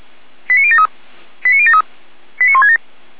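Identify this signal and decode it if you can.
Five-tone selective call (ZVEI), heard as quick runs of five tones, tones of about 70 ms. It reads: 89073 89073 89167
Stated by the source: ZVEI five-tone selective call 89119, received by SDR.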